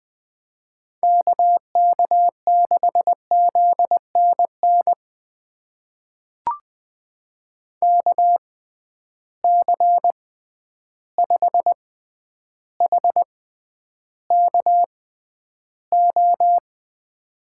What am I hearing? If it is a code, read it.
Morse code, heard as a keyed tone, about 20 wpm, spaced as for 10 wpm, 700 Hz. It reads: KK6ZNN KC5HKO
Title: USA Call Signs - 6 Characters - 6x Character Spacing - 20wpm